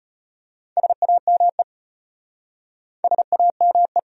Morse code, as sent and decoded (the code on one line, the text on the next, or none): ... .- -- . / ... .- -- .
SAME SAME